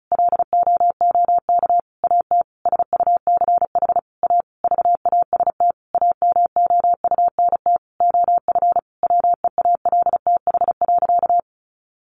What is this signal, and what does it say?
Morse code, heard as a keyed tone, about 34 wpm, 700 Hz. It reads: LOOK AT SUCH A VAST AMOUNT OF WEALTH.
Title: Sentences from Top 1300 Words 35wpm